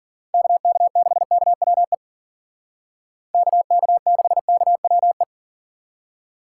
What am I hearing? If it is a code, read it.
Morse code, heard as a keyed tone, about 50 wpm, 700 Hz. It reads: KK6KWE KK6KWE